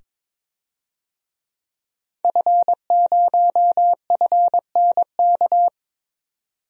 Morse code, heard as a keyed tone, about 22 wpm, 700 Hz. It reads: F0FNK
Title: DXCC-100 Call Signs - 5 Characters - 22wpm